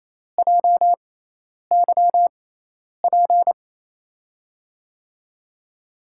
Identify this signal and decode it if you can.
Morse code, heard as a keyed tone, about 28 wpm, spaced as for 10 wpm, 700 Hz. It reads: JYP
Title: USA Call Signs - 6 Characters - 6x Character Spacing - 28wpm